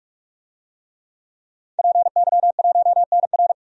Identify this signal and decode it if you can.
Morse code, heard as a keyed tone, about 46 wpm, 700 Hz. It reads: PY1NR
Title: DXCC-100 Call Signs - 5 Characters - 45wpm